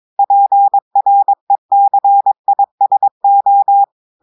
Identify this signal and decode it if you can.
Morse code, heard as a keyed tone, about 22 wpm, 800 Hz. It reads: PRECISO
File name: audios / portugais / portugais texte en clair 2 vitesse 22 mots minutes